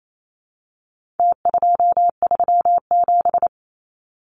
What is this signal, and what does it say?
Morse code, heard as a keyed tone, about 28 wpm, 700 Hz. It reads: T237
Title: Numbers - Contest Style - 4-Digit 28wpm